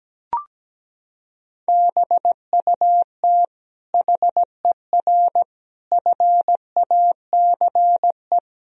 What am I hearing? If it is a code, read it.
Morse code, heard as a keyed tone, about 17 wpm, 700 Hz. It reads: BUT HER FACE
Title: Sentences from Top 1000 Words - Review Part 2 - 17wpm